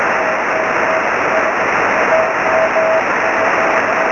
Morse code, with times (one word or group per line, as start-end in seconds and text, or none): none